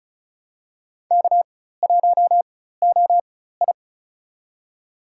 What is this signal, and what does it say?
Morse code, heard as a keyed tone, about 36 wpm, spaced as for 16 wpm, 700 Hz. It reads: K1OI
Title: USA Call Signs - 4x Character Spacing - 35wpm